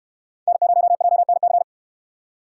Morse code, heard as a keyed tone, about 66 wpm, 700 Hz. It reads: N0JNC